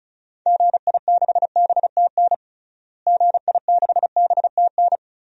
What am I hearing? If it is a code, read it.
Morse code, heard as a keyed tone, about 36 wpm, 700 Hz. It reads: GI6BTN GI6BTN
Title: DXCC-100 Call Signs - 6 Characters - Repeated 3x - 35wpm